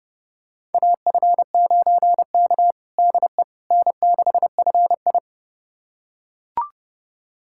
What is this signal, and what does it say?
Morse code, heard as a keyed tone, about 30 wpm, 700 Hz. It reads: AF9K DE N6FI
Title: USA Callsign DE Callsign - 4 Characters - 30wpm